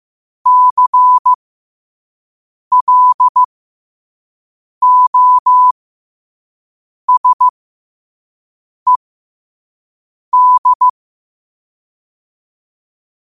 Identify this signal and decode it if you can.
Morse code, heard as a keyed tone, about 15 wpm, spaced as for 5 wpm, 1000 Hz. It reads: CLOSED
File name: qotdfiles / 5wpm